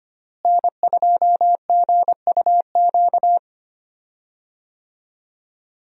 Morse code, heard as a keyed tone, about 25 wpm, 700 Hz. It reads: N2GUQ
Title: USA Call Signs - 5 Characters - Repeated 2x - 25wpm